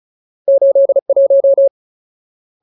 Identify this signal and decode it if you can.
Morse code, heard as a keyed tone, about 35 wpm, 550 Hz. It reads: 81
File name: Ultimate Word List / 35WPM Ultimate